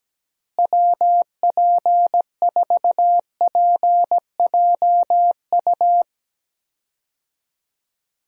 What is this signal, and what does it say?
Morse code, heard as a keyed tone, about 17 wpm, 700 Hz. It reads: WP4PJU
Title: USA Call Signs - 6 Characters - Repeated 2x - 17wpm